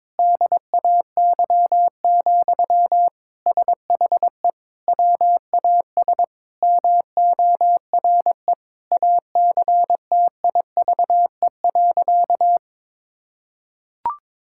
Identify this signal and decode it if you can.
Morse code, heard as a keyed tone, about 21 wpm, 700 Hz. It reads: DAY, SHE WAS MORE ACTIVE.
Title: Sentences from Top 1000 Words - Review Part 6 - 22wpm